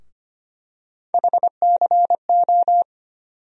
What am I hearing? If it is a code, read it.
Morse code, heard as a keyed tone, about 25 wpm, 700 Hz. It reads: HCO
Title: USA Call Sign Postfixes - 3 Letters 25wpm